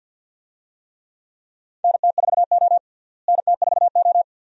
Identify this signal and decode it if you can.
Morse code, heard as a keyed tone, about 45 wpm, 700 Hz. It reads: NT4O NT4O
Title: USA Call Signs - 4 Characters - Repeated 2x - 50wpm